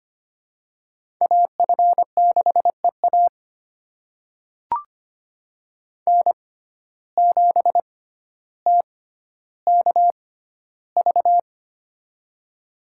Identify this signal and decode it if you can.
Morse code, heard as a keyed tone, about 25 wpm, spaced as for 13 wpm, 700 Hz. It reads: AF6EA N7TKV
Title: USA Call Signs - 5 Characters - 6x Character Spacing - 25wpm